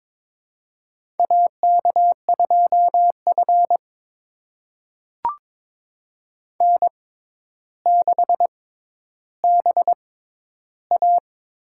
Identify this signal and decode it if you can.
Morse code, heard as a keyed tone, about 22 wpm, spaced as for 11 wpm, 700 Hz. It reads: AK2F N6BA
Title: USA Call Signs - 4 Characters - 6x Character Spacing - 22wpm